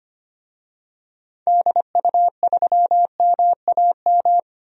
Morse code, heard as a keyed tone, about 25 wpm, 700 Hz. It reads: DU3MAM
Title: DXCC-100 Call Signs - 6 Characters - 25wpm